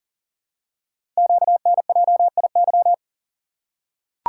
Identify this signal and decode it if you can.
Morse code, heard as a keyed tone, about 42 wpm, 700 Hz. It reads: QNJIY
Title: Code Groups - 5 Letters - 40wpm